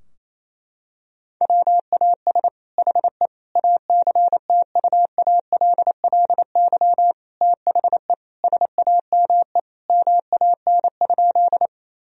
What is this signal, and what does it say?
Morse code, heard as a keyed tone, about 28 wpm, 700 Hz. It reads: WAS HE ACTUALLY THE SAME MAN?
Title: Sentences from Top 1000 Words - Review Part 4 - 28wpm